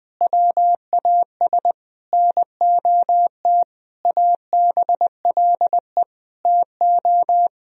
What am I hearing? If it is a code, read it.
Morse code, heard as a keyed tone, about 20 wpm, 700 Hz. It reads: WAS NOT ABLE TO
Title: Sentences from Top 1000 Words - Review Part 4 - 20wpm